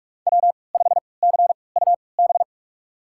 Morse code, heard as a keyed tone, about 46 wpm, spaced as for 26 wpm, 700 Hz. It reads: W5CUB
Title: USA Call Signs - 5 Characters - 3x Character Spacing - 45wpm